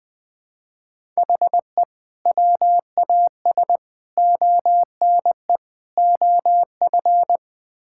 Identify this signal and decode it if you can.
Morse code, heard as a keyed tone, about 21 wpm, 700 Hz. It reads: HE WAS ONE OF
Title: Build A Phrase - 5-Word Phrases - 20wpm